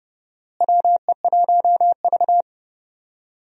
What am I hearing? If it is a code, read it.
Morse code, heard as a keyed tone, about 31 wpm, 700 Hz. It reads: WE1V
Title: USA Call Signs - 4 Characters - Repeated 3x - 30wpm